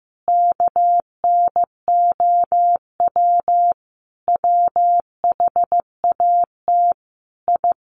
Morse code, heard as a keyed tone, about 15 wpm, 700 Hz. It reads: KNOW WHAT I
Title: Sets of 5 Words 15wpm